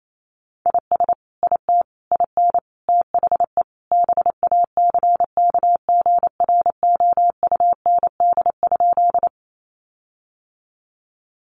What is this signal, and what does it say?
Morse code, heard as a keyed tone, about 28 wpm, 700 Hz. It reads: IS IT IN THE BACKGROUND?
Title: Sentences from Top 1300 Words 28wpm